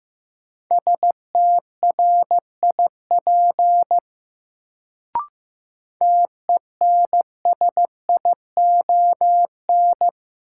Morse code, heard as a keyed tone, about 15 wpm, 700 Hz. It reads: STRIP TENSION
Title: Sentences from Top 1600 Words - New Word Drill Alt - 15wpm